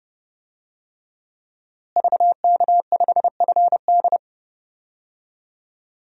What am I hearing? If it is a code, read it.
Morse code, heard as a keyed tone, about 32 wpm, 700 Hz. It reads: VK5FD
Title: DXCC-100 Call Signs - 5 Characters - Repeated 3x - 30wpm